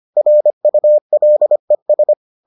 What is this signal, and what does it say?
Morse code, heard as a keyed tone, about 25 wpm, 600 Hz. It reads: RULES